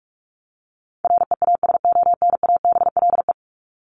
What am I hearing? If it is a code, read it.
Morse code, heard as a keyed tone, about 44 wpm, 700 Hz. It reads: REASONABLE
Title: Top 1501-1750 Words 45wpm